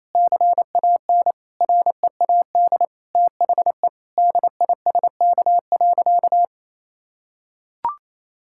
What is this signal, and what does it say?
Morse code, heard as a keyed tone, about 27 wpm, 700 Hz. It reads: CAN READ THE DISK.
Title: Sentences from Top 1000 Words - Review Part 6 - 28wpm